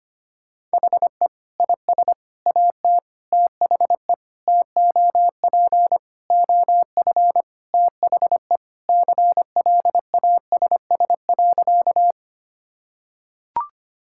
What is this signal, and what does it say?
Morse code, heard as a keyed tone, about 25 wpm, 700 Hz. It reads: HE IS AT THE TOP OF THE CLASS.
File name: practice / 25wpm / Sentences with 3-Word Phrases - 25wpm